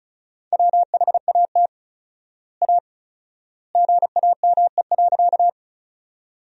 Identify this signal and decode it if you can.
Morse code, heard as a keyed tone, about 34 wpm, 700 Hz. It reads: WHAT A GAME.
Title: 4x Word-Spacing - Sentences from Top 500 Words - 35wpm